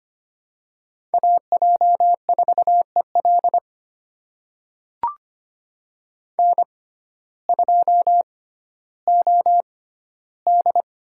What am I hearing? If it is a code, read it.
Morse code, heard as a keyed tone, about 25 wpm, spaced as for 14 wpm, 700 Hz. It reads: AJ4EL N2OD
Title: USA Call Signs - 6x Character Spacing - 25wpm